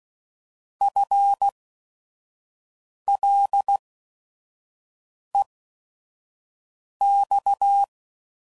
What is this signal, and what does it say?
Morse code, heard as a keyed tone, about 16 wpm, spaced as for 5 wpm, 800 Hz. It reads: FLEX
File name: Prefixes / By First Letter / Fs